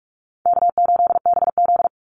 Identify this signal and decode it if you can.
Morse code, heard as a keyed tone, about 44 wpm, 700 Hz. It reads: K8BZ
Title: Easy Call Signs Repeated 3x - 45wpm